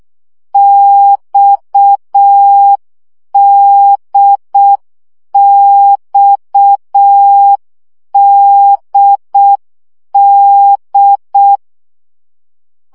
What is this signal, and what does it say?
Morse code, heard as a keyed tone, about 6 wpm, 800 Hz. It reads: XDXDD
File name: CW gyakorlas d b x 5 baud